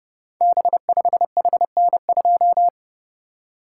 Morse code, heard as a keyed tone, about 30 wpm, 700 Hz. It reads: B5HN2
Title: Code Groups - 5 Letter:Numbers - 30wpm